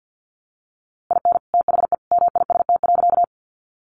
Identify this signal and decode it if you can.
Morse code, heard as a keyed tone, about 50 wpm, 700 Hz. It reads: IN THE MIST.
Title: A Sound Of Thunder 50wpm